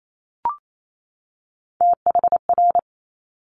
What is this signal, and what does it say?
Morse code, heard as a keyed tone, about 28 wpm, 700 Hz. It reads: THR
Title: Top Three Letter Combinations in Words 28wpm